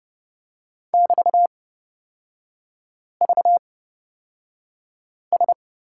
Morse code, heard as a keyed tone, about 29 wpm, spaced as for 5 wpm, 700 Hz. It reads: <BT>VS
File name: Lesson 36e - BT - Three Characters 30wpm